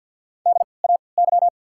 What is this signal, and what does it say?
Morse code, heard as a keyed tone, about 53 wpm, spaced as for 29 wpm, 700 Hz. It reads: DAY
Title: 3x Character Spacing - Top 100 Words - 50wpm